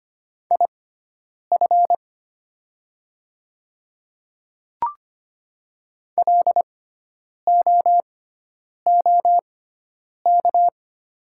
Morse code, heard as a keyed tone, about 25 wpm, spaced as for 9 wpm, 700 Hz. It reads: IF LOOK
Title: 6x Character Spacing - Top 100 Words - 25wpm